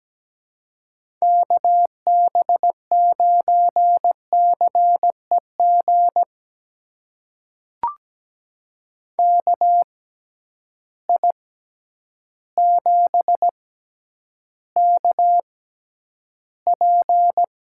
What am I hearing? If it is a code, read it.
Morse code, heard as a keyed tone, about 17 wpm, spaced as for 9 wpm, 700 Hz. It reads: KB9CEG KI7KP
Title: USA Call Signs - 6 Characters - 6x Character Spacing - 17wpm